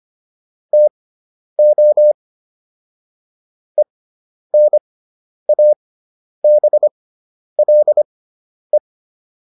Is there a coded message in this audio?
Yes: Morse code from 0.7 to 8.8 s, heard as a keyed tone, about 25 wpm, spaced as for 10 wpm, 600 Hz. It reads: TO ENABLE